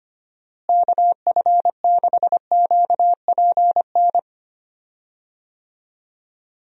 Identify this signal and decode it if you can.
Morse code, heard as a keyed tone, about 25 wpm, 700 Hz. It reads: KF6QPN
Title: USA Call Signs - 6 Characters - Repeated 2x - 25wpm